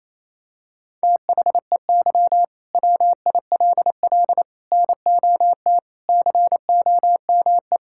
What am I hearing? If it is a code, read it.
Morse code, heard as a keyed tone, about 28 wpm, 700 Hz. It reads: THEY WILL NOT COME
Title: Sentences from Top 1000 Words - Review Part 5 - 28wpm